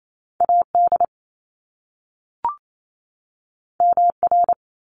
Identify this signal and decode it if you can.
Morse code, heard as a keyed tone, about 28 wpm, 700 Hz. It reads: AD MR
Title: Top Two Letter Combinations in Words 28wpm